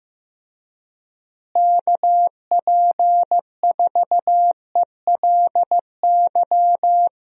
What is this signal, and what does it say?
Morse code, heard as a keyed tone, about 16 wpm, 700 Hz. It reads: KP4ELY
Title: USA Call Signs - 6 Characters - Repeated 3x - 15wpm